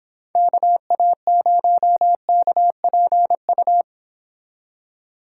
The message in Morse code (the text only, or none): KA0KPU